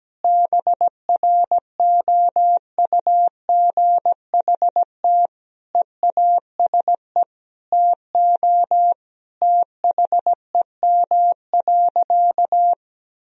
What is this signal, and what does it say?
Morse code, heard as a keyed tone, about 17 wpm, 700 Hz. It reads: BROUGHT EASE TO THEM.
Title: Sentences from Top 1000 Words - Review Part 1 - 17wpm